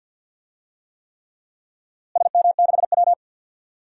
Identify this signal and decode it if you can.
Morse code, heard as a keyed tone, about 52 wpm, 700 Hz. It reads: SM6W
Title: DXCC-100 Call Signs - 4 Characters - Repeated 3x - 50wpm